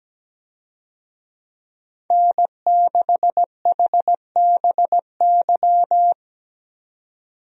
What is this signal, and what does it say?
Morse code, heard as a keyed tone, about 17 wpm, 700 Hz. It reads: N6HBY
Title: USA Call Signs - 5 Characters - 17wpm